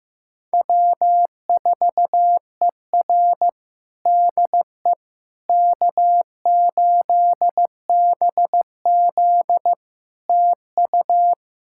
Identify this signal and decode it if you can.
Morse code, heard as a keyed tone, about 15 wpm, 700 Hz. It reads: W4ER DE K8BZ TU
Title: Practice QSO - Exchange 4 - Ending - 15wpm